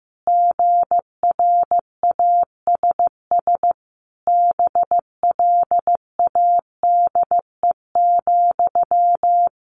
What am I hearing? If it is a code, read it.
Morse code, heard as a keyed tone, about 14 wpm, 700 Hz. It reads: GRASS BLADE,
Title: A Sound Of Thunder 15wpm